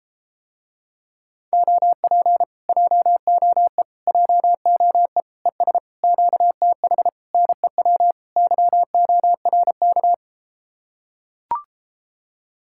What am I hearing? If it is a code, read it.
Morse code, heard as a keyed tone, about 33 wpm, 700 Hz. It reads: OP JOE JOE ES QTH NEW YORK